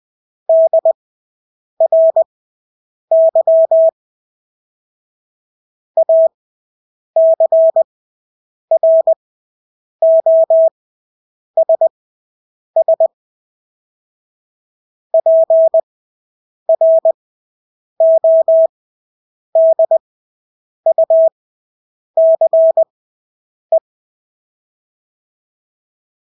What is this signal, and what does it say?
Morse code, heard as a keyed tone, about 20 wpm, spaced as for 8 wpm, 650 Hz. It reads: DRY ACROSS PRODUCE
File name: Engelske ord 20/8 wpm